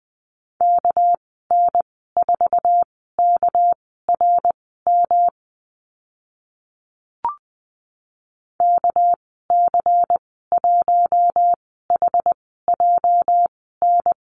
Morse code, heard as a keyed tone, about 20 wpm, spaced as for 14 wpm, 700 Hz. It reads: KN4KRM KC1HJN
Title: 213 Lesson 34g - Number 8 - Call Signs 20wpm